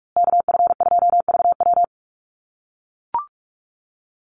Morse code, heard as a keyed tone, about 44 wpm, 700 Hz. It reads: KF2VW
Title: USA Call Signs Repeated 3x 45wpm